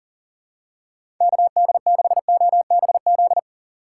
Morse code, heard as a keyed tone, about 40 wpm, 700 Hz. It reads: KD6OBZ
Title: USA Call Signs - 6x Character Spacing - 40wpm